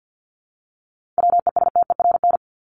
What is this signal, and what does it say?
Morse code, heard as a keyed tone, about 50 wpm, 700 Hz. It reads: WESTERN
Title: Top 751-1000 Words 50wpm